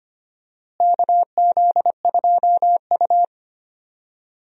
Morse code, heard as a keyed tone, about 25 wpm, 700 Hz. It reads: KZ2U
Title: USA Call Signs - 4 Characters - Repeated 2x - 25wpm